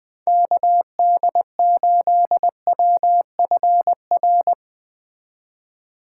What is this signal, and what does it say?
Morse code, heard as a keyed tone, about 20 wpm, 700 Hz. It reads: KD8WFR